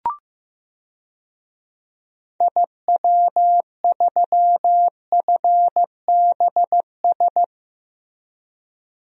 Morse code, heard as a keyed tone, about 15 wpm, 700 Hz. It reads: IW3FBS
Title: DXCC-100 Call Signs - 6 Characters - Repeated 2x - 15wpm